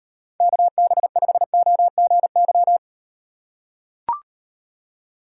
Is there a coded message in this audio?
Yes: Morse code from 0.4 to 2.8 s, heard as a keyed tone, about 38 wpm, 700 Hz. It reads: KB5OGY